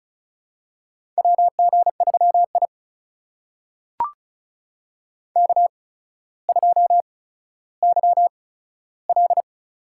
Morse code, heard as a keyed tone, about 36 wpm, spaced as for 15 wpm, 700 Hz. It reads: WG3I K2YL
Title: USA Call Signs - 4 Characters - 8x Character Spacing - 35wpm